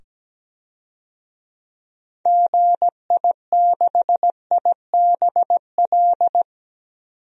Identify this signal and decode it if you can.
Morse code, heard as a keyed tone, about 17 wpm, 700 Hz. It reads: GI6IBL
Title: DXCC-100 Call Signs - 6 Characters - 17wpm